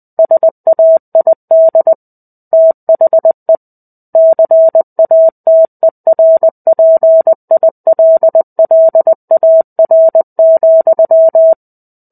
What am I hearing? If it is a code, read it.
Morse code, heard as a keyed tone, about 20 wpm, 650 Hz. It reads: SAID THE CATERPILLAR,
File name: 05-Chapter-05 20 WPM